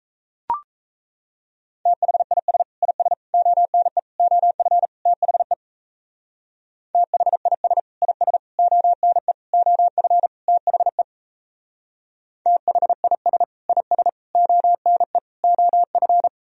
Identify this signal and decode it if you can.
Morse code, heard as a keyed tone, about 34 wpm, 700 Hz. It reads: THIS IS ONE OF THE THIS IS ONE OF THE THIS IS ONE OF
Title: Speed-Racing - Sets of 5 Words 28wpm